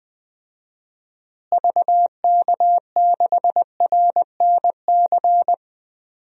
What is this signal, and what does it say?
Morse code, heard as a keyed tone, about 20 wpm, 700 Hz. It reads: VK6RNC
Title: DXCC-100 Call Signs - 6 Characters - Repeated 3x - 20wpm